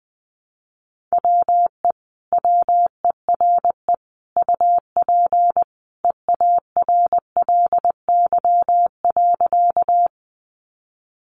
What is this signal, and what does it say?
Morse code, heard as a keyed tone, about 20 wpm, 700 Hz. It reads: WE WERE UP EARLY.
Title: Sentences from Top 400 Words 20wpm